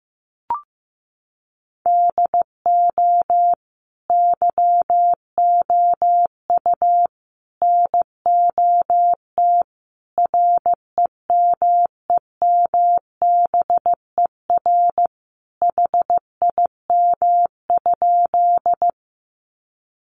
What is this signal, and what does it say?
Morse code, heard as a keyed tone, about 15 wpm, 700 Hz. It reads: DO YOU NOT REMEMBER HIM?
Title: Sentences from Top 600 Words 15wpm